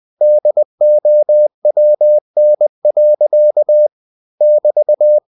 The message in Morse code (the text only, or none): DOWN. <BT>